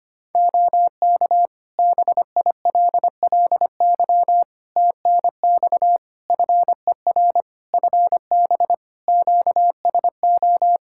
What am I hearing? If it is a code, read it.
Morse code, heard as a keyed tone, about 25 wpm, 700 Hz. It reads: OK BILLY TNX FER FB QSO
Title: Practice QSO - Exchange 4 - Ending - 25wpm